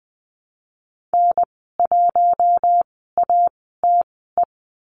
Morse code, heard as a keyed tone, about 20 wpm, spaced as for 15 wpm, 700 Hz. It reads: N1ATE
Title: 048 Lesson 9g - Number 4 - Call Signs 20wpm